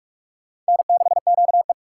Morse code, heard as a keyed tone, about 57 wpm, 700 Hz. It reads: N6QE